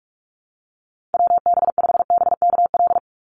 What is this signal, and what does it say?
Morse code, heard as a keyed tone, about 46 wpm, 700 Hz. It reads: WB5BKL
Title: NA SOTA Chaser Callsigns 45wpm